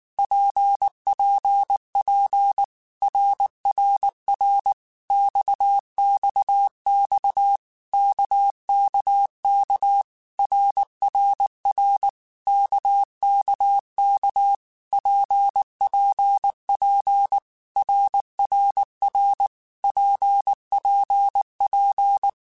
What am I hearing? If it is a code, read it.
Morse code, heard as a keyed tone, about 19 wpm, 800 Hz. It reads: PPP RRR XXX KKK RRR KKK PPP RRR PPP
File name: Basic Letter Groups By7Groups / RKPX / RKPX@18